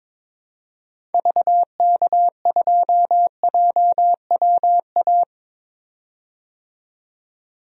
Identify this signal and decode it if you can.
Morse code, heard as a keyed tone, about 22 wpm, 700 Hz. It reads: VK2JWA
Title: DXCC-100 Call Signs - 6 Characters - 22wpm